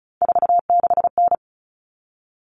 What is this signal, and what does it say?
Morse code, heard as a keyed tone, about 34 wpm, 700 Hz. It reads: 46N
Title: All Cut Numbers - 3-Digit 35wpm